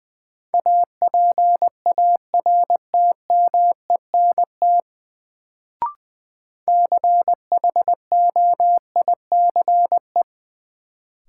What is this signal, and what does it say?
Morse code, heard as a keyed tone, about 21 wpm, 700 Hz. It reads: APARTMENT CHOICE
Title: Sentences from Top 900 Words - New Word Drill - 20wpm